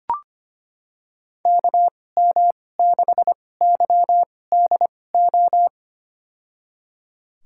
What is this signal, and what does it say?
Morse code, heard as a keyed tone, about 25 wpm, spaced as for 18 wpm, 700 Hz. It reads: KM6YDO
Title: USA Call Signs - 6 Characters - 2x Character Spacing - 25wpm